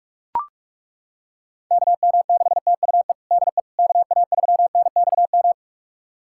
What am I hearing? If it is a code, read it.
Morse code, heard as a keyed tone, about 41 wpm, 700 Hz. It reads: KM6TUE DE KA3NXM
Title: USA Callsign DE Callsign 45wpm